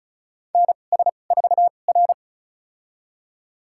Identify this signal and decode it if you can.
Morse code, heard as a keyed tone, about 36 wpm, spaced as for 25 wpm, 700 Hz. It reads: NS4R
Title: USA Call Signs - 4 Characters - 2x Character Spacing - 35wpm